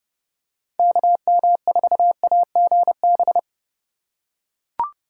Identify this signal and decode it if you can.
Morse code, heard as a keyed tone, about 30 wpm, 700 Hz. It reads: KM4AGB